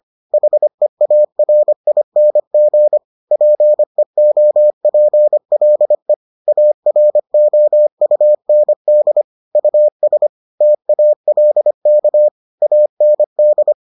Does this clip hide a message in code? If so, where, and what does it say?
Yes: Morse code from 0.3 to 13.7 s, heard as a keyed tone, about 25 wpm, 600 Hz. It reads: HEARING PEOPLE AROUND US TALK AND